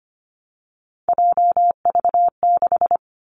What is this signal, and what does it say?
Morse code, heard as a keyed tone, about 25 wpm, 700 Hz. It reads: JV6
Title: Three Challenging Letter-Number Combinations 25wpm